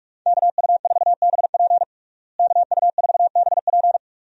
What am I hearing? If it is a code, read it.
Morse code, heard as a keyed tone, about 46 wpm, 700 Hz. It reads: KU4BP KU4BP
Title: USA Call Signs - 5 Characters - Repeated 2x - 45wpm